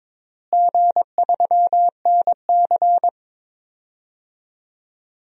Morse code, heard as a keyed tone, about 22 wpm, 700 Hz. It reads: G3NC
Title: DXCC-100 Call Signs - 4 Characters - Repeated 3x - 22wpm